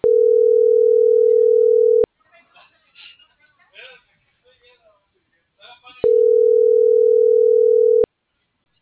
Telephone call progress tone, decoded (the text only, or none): ringback tone